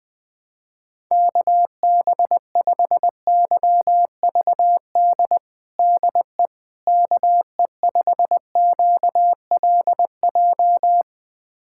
Morse code, heard as a keyed tone, about 20 wpm, 700 Hz. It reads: KB5YVD DE KE5QLJ
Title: USA Callsign DE Callsign - 6 Characters - 20wpm